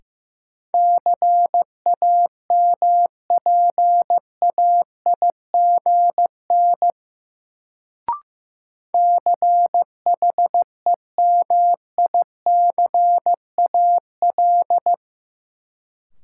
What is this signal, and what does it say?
Morse code, heard as a keyed tone, about 15 wpm, 700 Hz. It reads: CAMPAIGN CHEMICAL
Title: Sentences from Top 1200 Words - New Word Drill Alt - 15wpm